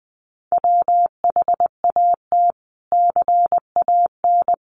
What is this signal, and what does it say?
Morse code, heard as a keyed tone, about 20 wpm, 700 Hz. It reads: WHAT CAN
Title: Sets of 5 Words Encore 20wpm